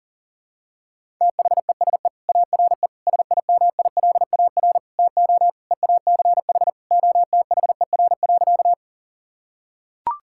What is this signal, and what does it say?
Morse code, heard as a keyed tone, about 37 wpm, 700 Hz. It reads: THESE ARE SIMILAR TO EACH OTHER.
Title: Sentences from Top 1000 Words - Review Part 5 - 40wpm